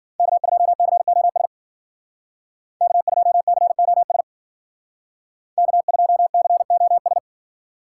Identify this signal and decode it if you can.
Morse code, heard as a keyed tone, about 52 wpm, 700 Hz. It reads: K2COS K2COS K2COS